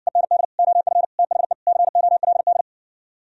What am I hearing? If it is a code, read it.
Morse code, heard as a keyed tone, about 67 wpm, 700 Hz. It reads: END OF THE COLD